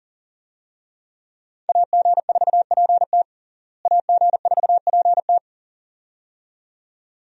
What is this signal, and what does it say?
Morse code, heard as a keyed tone, about 37 wpm, 700 Hz. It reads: AG4PT AG4PT
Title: USA Call Signs - 5 Characters - Repeated 2x - 40wpm